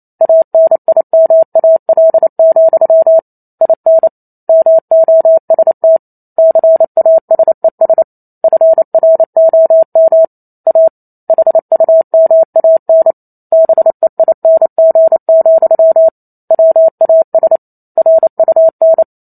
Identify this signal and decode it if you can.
Morse code, heard as a keyed tone, about 28 wpm, 650 Hz. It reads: ANIMAL, IN MOST CASES FROM A HUMAN BEING, WAS RUN
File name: Chapter 2 - 28 WPM